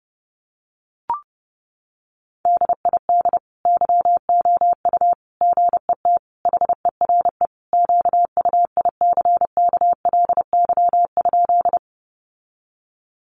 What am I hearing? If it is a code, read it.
Morse code, heard as a keyed tone, about 30 wpm, 700 Hz. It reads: DID YOU GET HERE QUICKLY?
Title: Sentences from Top 700 Words 30wpm